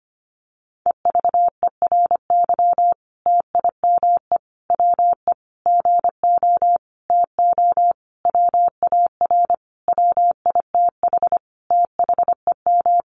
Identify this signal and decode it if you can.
Morse code, heard as a keyed tone, about 25 wpm, 700 Hz. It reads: EVERY TIME WE GO TO WAR WITH THEM
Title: Sentences from Top 1300 Words 25wpm